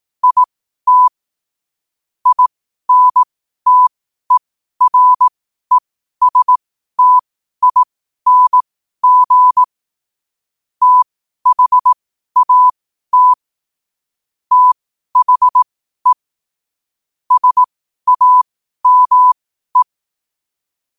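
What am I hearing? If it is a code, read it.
Morse code, heard as a keyed tone, about 18 wpm, spaced as for 12 wpm, 1000 Hz. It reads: IT INTERESTING THAT THE SAME
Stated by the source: Morse code quote of the day at 12 wpm.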